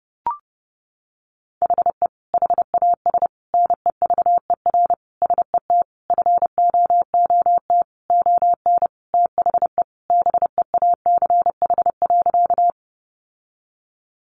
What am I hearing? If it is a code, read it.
Morse code, heard as a keyed tone, about 30 wpm, 700 Hz. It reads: HE HAS NEVER SET FOOT ON THE BEACH.
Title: Sentences from Top 1500 Words 30wpm